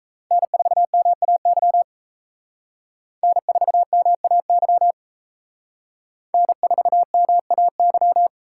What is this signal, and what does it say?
Morse code, heard as a keyed tone, about 36 wpm, 700 Hz. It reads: N4MAY N4MAY N4MAY